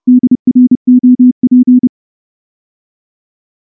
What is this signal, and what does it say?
Morse code, heard as a keyed tone, about 29 wpm, 250 Hz. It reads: DROP
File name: morse / morse-training-30WPM